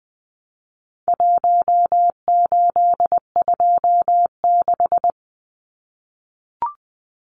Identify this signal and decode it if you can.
Morse code, heard as a keyed tone, about 20 wpm, 700 Hz. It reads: 1826
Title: Numbers - Contest Style - 4-Digit 20wpm